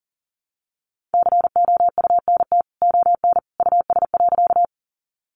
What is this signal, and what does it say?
Morse code, heard as a keyed tone, about 40 wpm, 700 Hz. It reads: COUNT ON US.
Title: Sentences from Top 1500 Words 40wpm